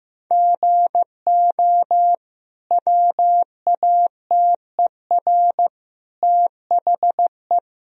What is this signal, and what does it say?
Morse code, heard as a keyed tone, about 15 wpm, 700 Hz. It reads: GO WATER THE
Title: Sentences from Top 1000 Words - Review Part 1 - 15wpm